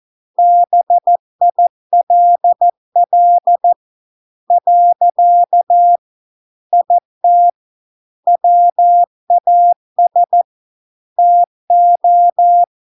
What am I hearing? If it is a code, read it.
Morse code, heard as a keyed tone, about 14 wpm, 700 Hz. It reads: BILL . IT WAS TO